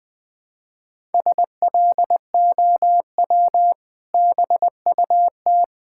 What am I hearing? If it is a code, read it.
Morse code, heard as a keyed tone, about 20 wpm, 700 Hz. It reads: SLOW BUT
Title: Sentences with 2-Word Phrases - 20wpm